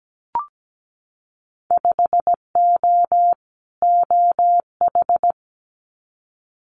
Morse code, heard as a keyed tone, about 17 wpm, 700 Hz. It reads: 5O OH
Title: Winter Field Day Exchanges Encore 17wpm